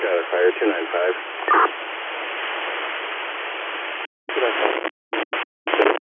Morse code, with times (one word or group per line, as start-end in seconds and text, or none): none